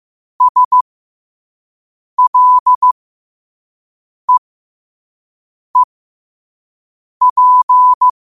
Morse code, heard as a keyed tone, about 15 wpm, spaced as for 5 wpm, 1000 Hz. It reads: SLEEP
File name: 5wpm